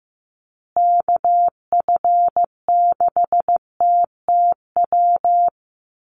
Morse code, heard as a keyed tone, about 15 wpm, 700 Hz. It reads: KF6TTW
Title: USA Call Signs Repeated 3x 15wpm